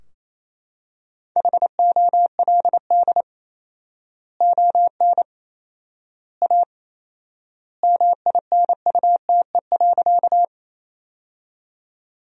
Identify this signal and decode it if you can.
Morse code, heard as a keyed tone, about 28 wpm, 700 Hz. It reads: HOLD ON A MINUTE.
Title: 4x Word-Spacing - Sentences from Top 500 Words - 28wpm